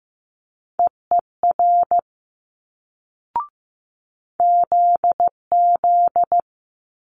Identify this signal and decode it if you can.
Morse code, heard as a keyed tone, about 15 wpm, 700 Hz. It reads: EER ZZ
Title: Multi-letter Phonemes Quick Fire 15wpm